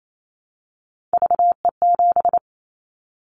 Morse code, heard as a keyed tone, about 28 wpm, 700 Hz. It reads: VE7
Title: Callsign Prefixes for Top 100 DXCC - 28wpm